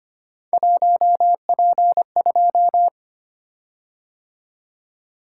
Morse code, heard as a keyed tone, about 25 wpm, 700 Hz. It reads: 1P2
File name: Three Letter-Number Combinations - 25wpm